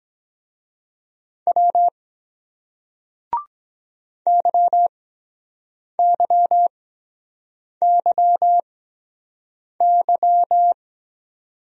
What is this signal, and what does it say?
Morse code, heard as a keyed tone, about 23 wpm, spaced as for 7 wpm, 700 Hz. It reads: W YYYY